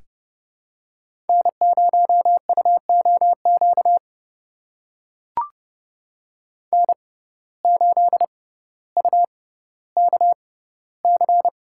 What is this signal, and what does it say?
Morse code, heard as a keyed tone, about 32 wpm, spaced as for 16 wpm, 700 Hz. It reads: N0UOQ N8UKC
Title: USA Call Signs - 5 Characters - 6x Character Spacing - 30wpm